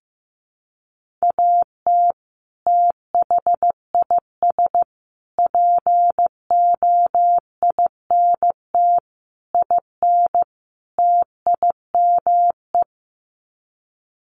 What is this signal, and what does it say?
Morse code, heard as a keyed tone, about 15 wpm, 700 Hz. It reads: AT THIS POINT IN TIME
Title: Sets of 5 Words Encore 15wpm